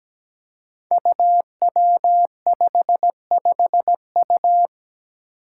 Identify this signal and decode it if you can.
Morse code, heard as a keyed tone, about 17 wpm, 700 Hz. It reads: UW55U
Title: Five Challenging Letter-Number Combinations 17wpm